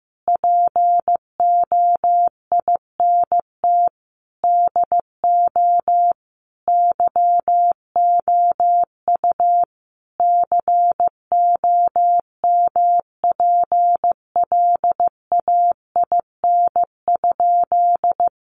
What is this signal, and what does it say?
Morse code, heard as a keyed tone, about 15 wpm, 700 Hz. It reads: POINT DO YOU COMPLAIN?
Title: Sentences from Top 1300 Words 15wpm